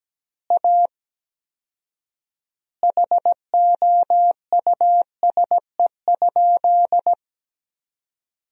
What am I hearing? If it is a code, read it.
Morse code, heard as a keyed tone, about 17 wpm, 700 Hz. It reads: A HOUSE?
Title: 4x Word-Spacing - Sentences from Top 200 Words - 17wpm